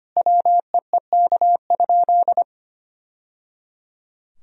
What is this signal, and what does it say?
Morse code, heard as a keyed tone, about 25 wpm, 700 Hz. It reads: WEEK?
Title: Sentences from Top 1000 Words - Review Part 3 - 25wpm